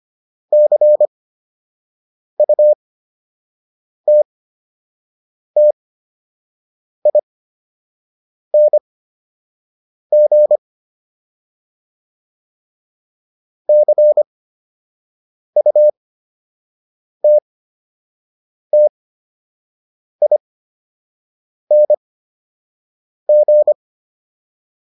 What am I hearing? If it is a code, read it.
Morse code, heard as a keyed tone, about 25 wpm, spaced as for 6 wpm, 600 Hz. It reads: CUTTING CUTTING